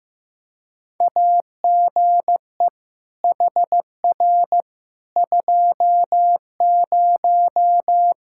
Morse code, heard as a keyed tone, about 15 wpm, 700 Hz. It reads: AGE HR 20
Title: K1USN SST QSO Practice 15wpm